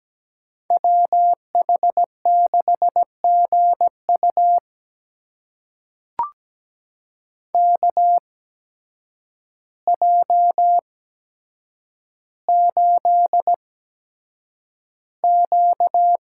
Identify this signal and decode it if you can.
Morse code, heard as a keyed tone, about 17 wpm, spaced as for 8 wpm, 700 Hz. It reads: WH6GU KJ8Q
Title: USA Call Signs - 8x Character Spacing - 17wpm